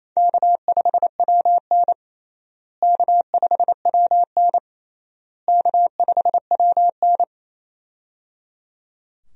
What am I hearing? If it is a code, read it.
Morse code, heard as a keyed tone, about 27 wpm, 700 Hz. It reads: K5WN K5WN K5WN
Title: USA Call Signs - 4 Characters - Repeated 3x - 28wpm